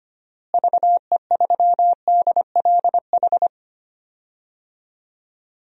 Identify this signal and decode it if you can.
Morse code, heard as a keyed tone, about 25 wpm, 700 Hz. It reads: VE3DLH
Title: DXCC-100 Call Signs - 6 Characters - Repeated 2x - 25wpm